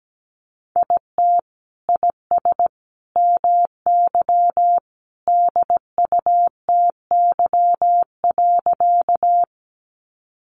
Morse code, heard as a keyed tone, about 17 wpm, 700 Hz. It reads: IT IS MY DUTY.
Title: Sentences from Top 1500 Words 17wpm